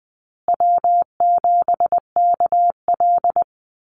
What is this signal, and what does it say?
Morse code, heard as a keyed tone, about 20 wpm, 700 Hz. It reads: W7KL
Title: USA Call Signs Repeated 2x 20wpm